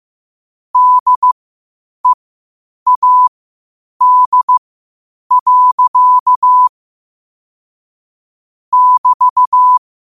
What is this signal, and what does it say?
Morse code, heard as a keyed tone, about 15 wpm, spaced as for 8 wpm, 1000 Hz. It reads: DEAD. <BT>
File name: qotdfiles / 8wpm